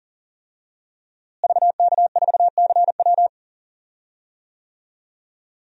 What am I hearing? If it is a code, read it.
Morse code, heard as a keyed tone, about 40 wpm, 690 Hz. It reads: VK4CW
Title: DXCC-100 Call Signs - 5 Characters - Repeated 2x - 40wpm